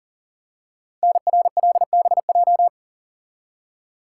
Morse code, heard as a keyed tone, about 39 wpm, 700 Hz. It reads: NRLBJ
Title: Code Groups - 5 Letters - 40wpm